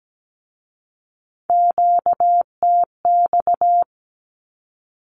Morse code, heard as a keyed tone, about 17 wpm, 700 Hz. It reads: QTX
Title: Q-Codes with Spoken Meaning - 17wpm